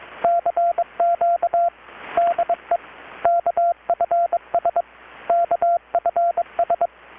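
Morse code, heard as a keyed tone, about 22 wpm, 690 Hz, with radio noise. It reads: CQ DE KFS KFS
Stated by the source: Morse from KPH's Night of Nights XX maritime radiotelegraphy event, received in Chiapas, Mexico about 3,800 km away on a Kenwood R-600 with a grounded 10 m long-wire antenna.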